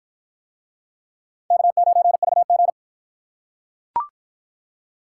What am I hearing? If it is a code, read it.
Morse code, heard as a keyed tone, about 45 wpm, 700 Hz. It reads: K9VG